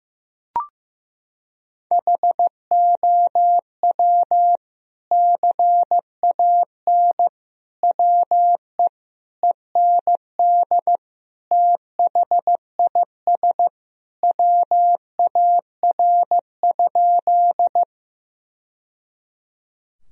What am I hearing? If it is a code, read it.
Morse code, heard as a keyed tone, about 15 wpm, 700 Hz. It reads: HOW CAN WE END THIS WAR?
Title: Sentences from Top 1000 Words - Review Part 2 - 15wpm